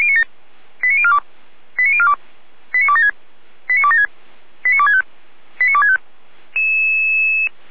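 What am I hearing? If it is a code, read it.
Five-tone selective call (ZVEI), heard as quick runs of five tones, tones of about 70 ms. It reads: 89042 89042 89276 89276 89265 89265
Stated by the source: ZVEI five-tone call 89294 received by SDR.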